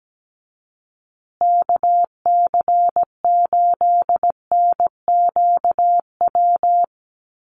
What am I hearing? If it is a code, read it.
Morse code, heard as a keyed tone, about 17 wpm, 700 Hz. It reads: KC8NQW
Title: USA Call Signs Repeated 3x 17wpm